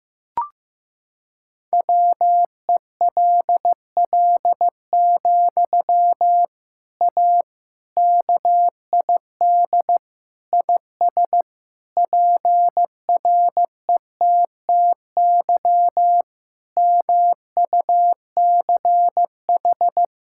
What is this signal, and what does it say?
Morse code, heard as a keyed tone, about 15 wpm, 700 Hz. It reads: WELL, A KID IS PRETTY MUCH
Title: Sentences from Top 1600 Words 15wpm